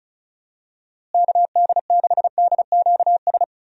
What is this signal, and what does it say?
Morse code, heard as a keyed tone, about 36 wpm, 700 Hz. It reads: KD6DQS
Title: USA Call Signs - 6 Characters - Repeated 2x - 35wpm